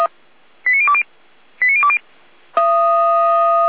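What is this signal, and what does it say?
Five-tone selective call (ZVEI), heard as quick runs of five tones, tones of about 70 ms. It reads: 89020 89020
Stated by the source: Five-tone ZVEI selective call received on an SDR.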